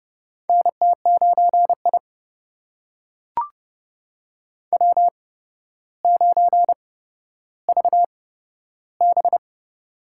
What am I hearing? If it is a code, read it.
Morse code, heard as a keyed tone, about 32 wpm, spaced as for 13 wpm, 700 Hz. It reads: NT9I W9VB
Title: USA Call Signs - 4 Characters - 8x Character Spacing - 30wpm